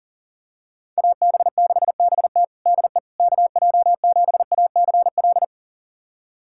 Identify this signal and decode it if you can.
Morse code, heard as a keyed tone, about 40 wpm, 700 Hz. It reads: AB6BT DE KJ7ACL
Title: USA Callsign DE Callsign 40wpm